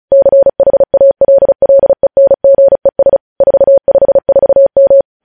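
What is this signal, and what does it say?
Morse code, heard as a keyed tone, about 36 wpm, 550 Hz. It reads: CHALLENGES 454M
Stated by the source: News headlines sent in Morse at 35 WPM.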